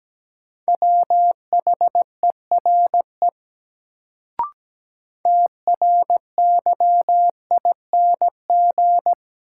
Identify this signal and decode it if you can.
Morse code, heard as a keyed tone, about 17 wpm, 700 Hz. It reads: WHERE TRYING
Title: Sets of 2 Words - New Word Drill - 17wpm